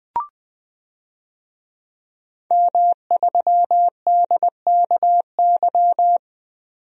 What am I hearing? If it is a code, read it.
Morse code, heard as a keyed tone, about 20 wpm, 700 Hz. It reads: M3DKY
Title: DXCC-100 Call Signs - 5 Characters - 20wpm